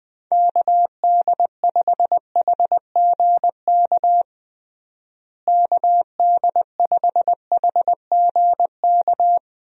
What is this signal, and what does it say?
Morse code, heard as a keyed tone, about 20 wpm, 700 Hz. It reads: KD5HGK KD5HGK
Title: USA Call Signs - 6 Characters - Repeated 3x - 20wpm